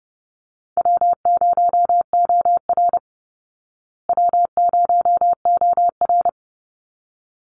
Morse code, heard as a keyed tone, about 30 wpm, 700 Hz. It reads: W0OR W0OR
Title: Call Signs Repeated 3x - 30wpm